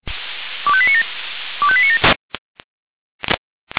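Five-tone selective call (ZVEI), heard as quick runs of five tones, tones of about 70 ms. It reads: 25897 25897
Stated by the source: ZVEI five-tone selective call received on an SDR.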